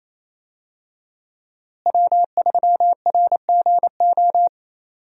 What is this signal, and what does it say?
Morse code, heard as a keyed tone, about 28 wpm, 700 Hz. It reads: W3RGO
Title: USA Call Signs - 5 Characters - Repeated 3x - 28wpm